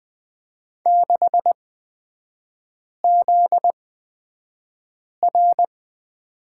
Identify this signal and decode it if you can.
Morse code, heard as a keyed tone, about 20 wpm, spaced as for 5 wpm, 700 Hz. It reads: 6ZR
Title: 243 Lesson 39e - Letter z - Three Characters 20wpm